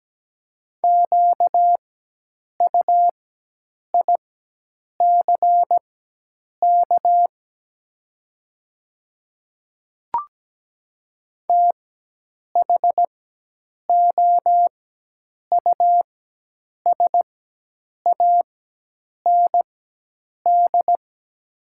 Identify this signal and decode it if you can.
Morse code, heard as a keyed tone, about 17 wpm, spaced as for 8 wpm, 700 Hz. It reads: QUICK THOUSAND
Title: 4x Character Spacing - Top 500 Words - 17wpm